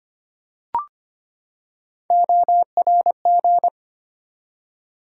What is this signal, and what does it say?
Morse code, heard as a keyed tone, about 25 wpm, 700 Hz. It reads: ORG
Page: morse-code-ninja-practice-25wpm